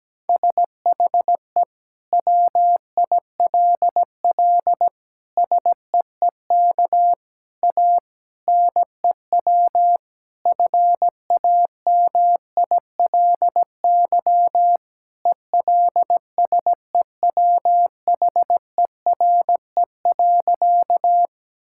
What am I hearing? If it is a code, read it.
Morse code, heard as a keyed tone, about 17 wpm, 700 Hz. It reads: SHE WILL SEEK A NEW FAMILY ELSEWHERE.
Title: Sentences from Top 1600 Words 17wpm